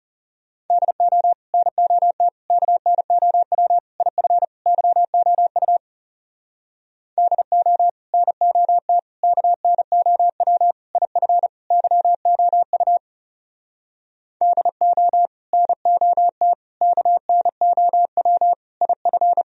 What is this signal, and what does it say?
Morse code, heard as a keyed tone, about 40 wpm, 700 Hz. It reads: DO NOT KNOW IF YOU DO NOT KNOW IF YOU DO NOT KNOW IF